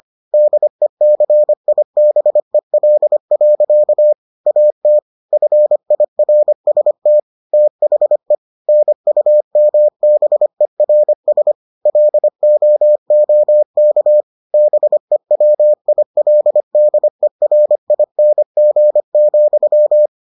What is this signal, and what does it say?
Morse code, heard as a keyed tone, about 25 wpm, 600 Hz. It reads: DECIBEL. AT FIRST THE NUMBERS LOOK BEWILDERING,